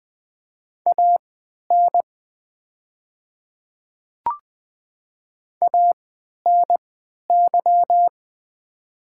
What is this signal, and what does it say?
Morse code, heard as a keyed tone, about 20 wpm, spaced as for 11 wpm, 690 Hz. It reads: AN ANY
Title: 3x Character Spacing - Top 200 Words - 20wpm